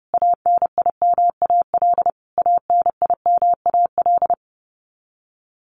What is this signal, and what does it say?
Morse code, heard as a keyed tone, about 30 wpm, 700 Hz. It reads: ANIMAL ANIMAL
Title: Top 500 Words Repeated 30wpm